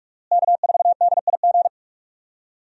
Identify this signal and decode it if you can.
Morse code, heard as a keyed tone, about 44 wpm, 700 Hz. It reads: K4DIG
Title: USA Call Signs - 5 Characters - Repeated 2x - 45wpm